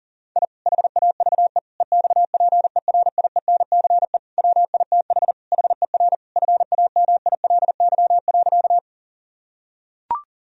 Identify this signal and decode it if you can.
Morse code, heard as a keyed tone, about 40 wpm, 700 Hz. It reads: I HAVE EXPERIENCE WITH HER FAMILY.
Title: Sentences from Top 1000 Words - Review Part 3 - 40wpm